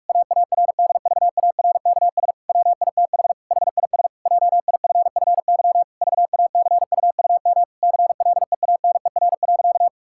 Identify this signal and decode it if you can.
Morse code, heard as a keyed tone, about 46 wpm, 700 Hz. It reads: AARDVARKS WITH HIS JIFFY VACUUM CLEANER.
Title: Pangram Challenge v2 - 45wpm